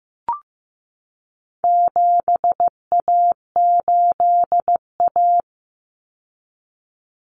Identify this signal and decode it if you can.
Morse code, heard as a keyed tone, about 15 wpm, 700 Hz. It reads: 7A8A
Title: Common Cut Numbers - 4-Digit 15wpm